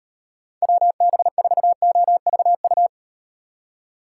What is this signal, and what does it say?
Morse code, heard as a keyed tone, about 38 wpm, 700 Hz. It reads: WB4OVU